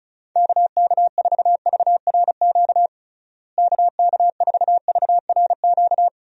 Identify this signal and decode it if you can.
Morse code, heard as a keyed tone, about 34 wpm, 700 Hz. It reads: KK4VRQ KK4VRQ
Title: USA Call Signs - 6 Characters - Repeated 3x - 35wpm